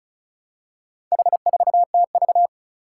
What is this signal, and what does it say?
Morse code, heard as a keyed tone, about 36 wpm, 700 Hz. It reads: H4TV
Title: Four Challenging Letter-Number Combinations 35wpm